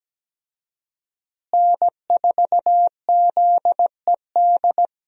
Morse code, heard as a keyed tone, about 17 wpm, 700 Hz. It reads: N4ZED
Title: USA Call Signs - 5 Characters - Repeated 3x - 17wpm